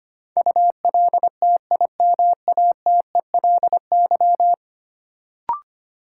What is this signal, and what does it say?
Morse code, heard as a keyed tone, about 25 wpm, 700 Hz. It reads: ULTIMATELY
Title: Sentences from Top 1200 Words - New Word Drill Alt - 25wpm